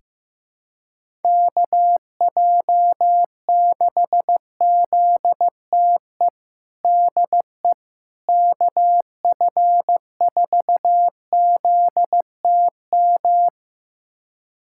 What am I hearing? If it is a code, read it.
Morse code, heard as a keyed tone, about 15 wpm, 700 Hz. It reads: KJ6ZTE DE KF4ZTM
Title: USA Callsign DE Callsign - 6 Characters - 15wpm